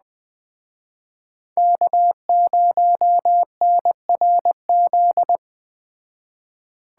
Morse code, heard as a keyed tone, about 20 wpm, 700 Hz. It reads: K0NRZ
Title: Speed Racing - USA 1x3 Call Signs 17wpm